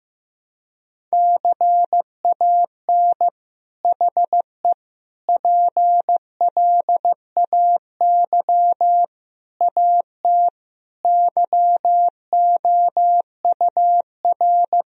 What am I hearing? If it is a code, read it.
Morse code, heard as a keyed tone, about 16 wpm, 700 Hz. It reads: CAN HE PLAY AT YOUR
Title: Sentences from Top 1000 Words - Review Part 4 - 15wpm